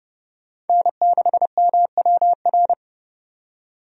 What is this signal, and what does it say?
Morse code, heard as a keyed tone, about 32 wpm, 700 Hz. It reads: N6MWR
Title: USA Call Signs - 5 Characters - 30wpm